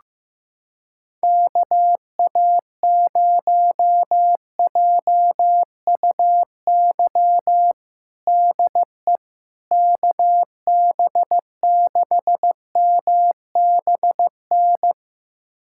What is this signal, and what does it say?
Morse code, heard as a keyed tone, about 15 wpm, 700 Hz. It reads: KA0JUY DE KB6MBN
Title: USA Callsign DE Callsign 15wpm